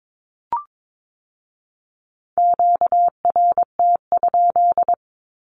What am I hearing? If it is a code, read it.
Morse code, heard as a keyed tone, about 22 wpm, 700 Hz. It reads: QRT?
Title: Q-Codes with Spoken Meaning - 22wpm